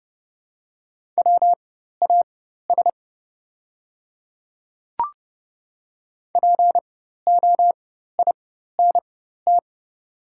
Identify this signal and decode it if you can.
Morse code, heard as a keyed tone, about 30 wpm, spaced as for 14 wpm, 700 Hz. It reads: WAS POINT
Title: 4x Character Spacing - Top 200 Words - 30wpm